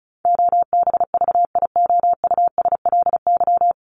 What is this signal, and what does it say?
Morse code, heard as a keyed tone, about 34 wpm, 700 Hz. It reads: OBVIOUSLY
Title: LY Suffix - Top 100 Words with Base Words 35wpm